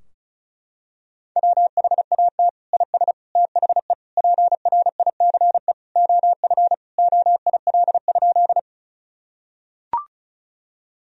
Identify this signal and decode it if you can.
Morse code, heard as a keyed tone, about 34 wpm, 700 Hz. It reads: WHAT IS THE PRICE OF OIL?
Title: Sentences from Top 1000 Words - Review Part 4 - 35wpm